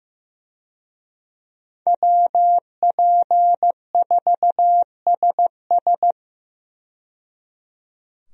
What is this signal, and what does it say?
Morse code, heard as a keyed tone, about 15 wpm, 700 Hz. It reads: WP4SS
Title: USA Call Signs - 5 Characters - 15wpm